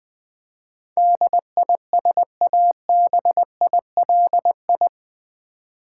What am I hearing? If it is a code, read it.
Morse code, heard as a keyed tone, about 20 wpm, 700 Hz. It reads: DISABILI
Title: Build A Word - 4 Letters At A Time - 10-Letter Words - 20wpm